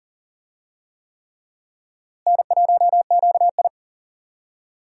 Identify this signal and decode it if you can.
Morse code, heard as a keyed tone, about 40 wpm, 700 Hz. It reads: N1QI
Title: USA Call Signs - 4 Characters - Repeated 3x - 40wpm